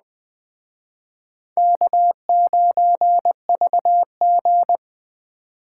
Morse code, heard as a keyed tone, about 20 wpm, 700 Hz. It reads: K9VG